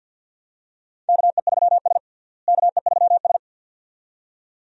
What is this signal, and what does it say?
Morse code, heard as a keyed tone, about 52 wpm, 700 Hz. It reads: KE3S KE3S
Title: USA Call Signs - 4 Characters - Repeated 2x - 50wpm